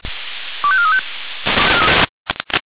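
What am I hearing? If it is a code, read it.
Five-tone selective call (ZVEI), heard as quick runs of five tones, tones of about 70 ms. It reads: 25435
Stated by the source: ZVEI five-tone selective call received by SDR.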